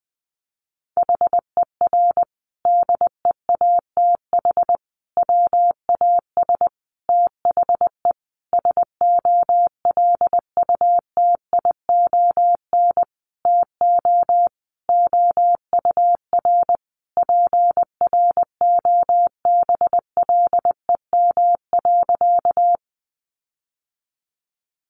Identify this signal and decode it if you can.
Morse code, heard as a keyed tone, about 20 wpm, 700 Hz. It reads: HER DEATH WAS THE SOLUTION TO OUR PROBLEM.
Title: Sentences from Top 1300 Words 20wpm